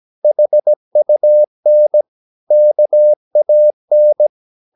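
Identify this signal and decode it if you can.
Morse code, heard as a keyed tone, about 17 wpm, 600 Hz. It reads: HUN KAN
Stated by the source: Slow (QRS) Morse practice at 17 wpm, from CW med Gnister.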